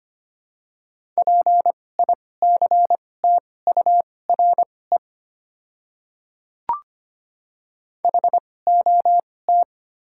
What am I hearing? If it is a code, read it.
Morse code, heard as a keyed tone, about 25 wpm, spaced as for 18 wpm, 700 Hz. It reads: PICTURE HOT
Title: 2x Character Spacing - Top 200 Words - 25wpm